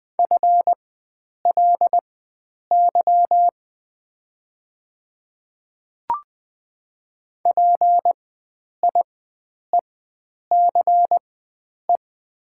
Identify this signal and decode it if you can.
Morse code, heard as a keyed tone, about 20 wpm, spaced as for 9 wpm, 700 Hz. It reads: FLY PIECE
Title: 4x Character Spacing - Top 500 Words - 20wpm